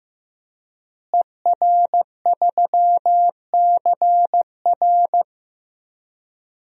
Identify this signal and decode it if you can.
Morse code, heard as a keyed tone, about 15 wpm, 700 Hz. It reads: ER3CR
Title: DXCC-100 Call Signs - 5 Characters - Repeated 2x - 15wpm